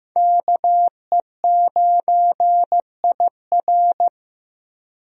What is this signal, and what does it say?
Morse code, heard as a keyed tone, about 16 wpm, 700 Hz. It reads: KE9IR
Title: USA Call Signs - 5 Characters - Repeated 2x - 15wpm